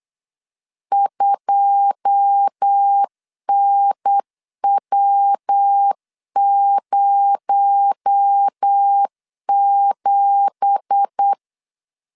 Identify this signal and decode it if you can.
Morse code, heard as a keyed tone, about 8 wpm, 800 Hz. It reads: 2NW07
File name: CW / DET / 1 9 1500C S C 8wpm 5653